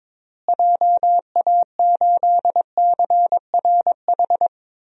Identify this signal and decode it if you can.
Morse code, heard as a keyed tone, about 21 wpm, 700 Hz. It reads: JA8CRH
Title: DXCC-100 Call Signs - 6 Characters - Repeated 3x - 22wpm